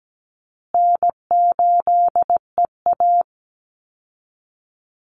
Morse code, heard as a keyed tone, about 17 wpm, 700 Hz. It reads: N8EA
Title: Call Signs Repeated 3x - 17wpm